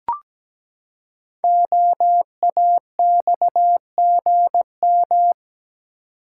Morse code, heard as a keyed tone, about 17 wpm, 700 Hz. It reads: OAXGM
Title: Code Groups - 5 Letters - 17wpm